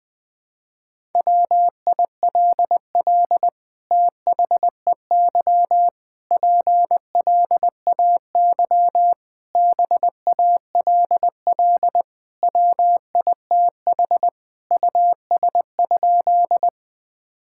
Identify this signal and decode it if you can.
Morse code, heard as a keyed tone, about 20 wpm, 700 Hz. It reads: WILL THEY PLAY BALL WITH US?
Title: Sentences from Top 1000 Words - Review Part 6 - 20wpm